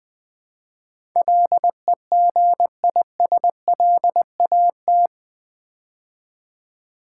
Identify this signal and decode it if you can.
Morse code, heard as a keyed tone, about 20 wpm, 700 Hz. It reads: LEGISLAT
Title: Build A Word - 4 Letters At A Time - 11-Letter Words - 20wpm